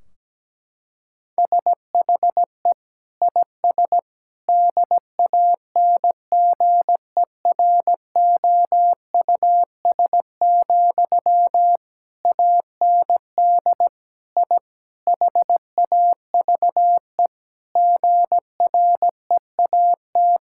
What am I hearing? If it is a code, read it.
Morse code, heard as a keyed tone, about 17 wpm, 700 Hz. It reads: SHE IS DANGEROUS, AND I HAVE GREAT
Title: Sentences from Top 1600 Words 17wpm